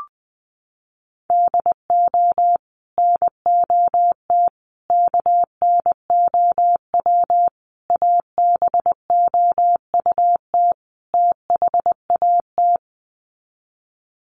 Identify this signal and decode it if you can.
Morse code, heard as a keyed tone, about 20 wpm, 700 Hz. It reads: DO NOT KNOW ABOUT THAT
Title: Sets of 5 Words Encore2 20wpm